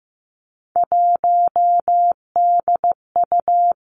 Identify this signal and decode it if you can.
Morse code, heard as a keyed tone, about 15 wpm, 700 Hz. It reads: 1DU
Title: Three Challenging Letter-Number Combinations 15wpm